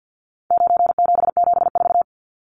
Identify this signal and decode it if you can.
Morse code, heard as a keyed tone, about 50 wpm, 700 Hz. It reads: 9774
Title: Numbers - 4-Digit 50wpm